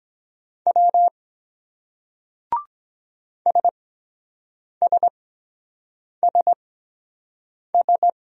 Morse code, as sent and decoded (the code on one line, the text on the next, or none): .-- / ... ... ... ...
W SSSS